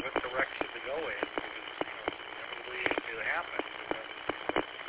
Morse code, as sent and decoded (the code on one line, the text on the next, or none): none